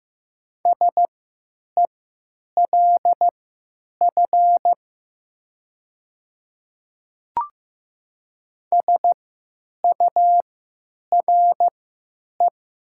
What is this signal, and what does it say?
Morse code, heard as a keyed tone, about 15 wpm, spaced as for 9 wpm, 700 Hz. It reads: SELF SURE
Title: 3x Character Spacing - Top 500 Words - 15wpm